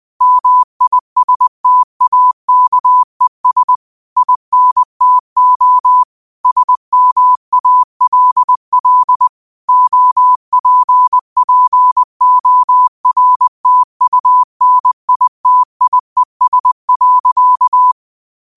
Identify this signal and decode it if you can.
Morse code, heard as a keyed tone, about 20 wpm, 1000 Hz. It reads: MISTAKES INTO SMALL OPPORTUNITIES.
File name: qotdfiles / 20wpm